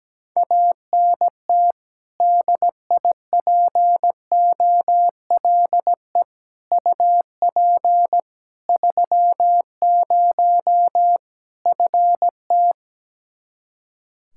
Morse code, heard as a keyed tone, about 17 wpm, 700 Hz. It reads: ANT DIPOLE UP 30 FT
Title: Practice QSO - Exchange 2 - Info - 17wpm